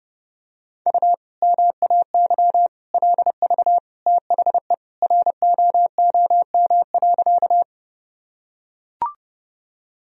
Morse code, heard as a keyed tone, about 32 wpm, 700 Hz. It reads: U MAY LV THE ROOM.
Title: Sentences from Top 400 Words and Abbreviations 30wpm